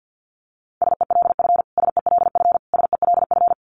Morse code, heard as a keyed tone, about 50 wpm, 700 Hz. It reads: SELF SELF SELF
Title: Top 200 Words Repeated 50wpm